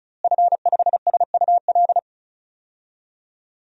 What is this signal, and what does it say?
Morse code, heard as a keyed tone, about 34 wpm, 700 Hz. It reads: F5SUL
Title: DXCC-100 Call Signs - 5 Characters - Repeated 3x - 35wpm